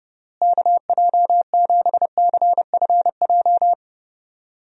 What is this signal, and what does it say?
Morse code, heard as a keyed tone, about 32 wpm, 700 Hz. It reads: KJ7CFJ
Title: USA Call Signs - 6 Characters - Repeated 2x - 30wpm